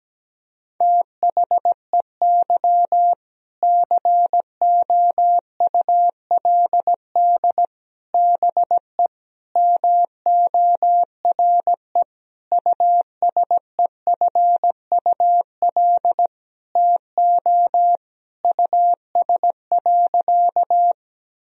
Morse code, heard as a keyed tone, about 17 wpm, 700 Hz. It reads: THEY COULD BE MORE USEFUL TO US.
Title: Sentences from Top 1000 Words - Review Part 4 - 17wpm